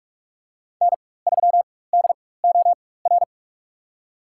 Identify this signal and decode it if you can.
Morse code, heard as a keyed tone, about 46 wpm, spaced as for 21 wpm, 700 Hz. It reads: N3DOR
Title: USA Call Signs - 5 Characters - 4x Character Spacing - 45wpm